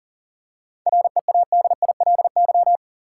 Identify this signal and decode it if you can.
Morse code, heard as a keyed tone, about 40 wpm, 700 Hz. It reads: READILY
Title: Seven Letter Words - 40wpm